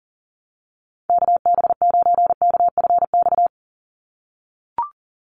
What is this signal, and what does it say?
Morse code, heard as a keyed tone, about 40 wpm, 700 Hz. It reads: KB9KFX
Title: USA Call Signs 40wpm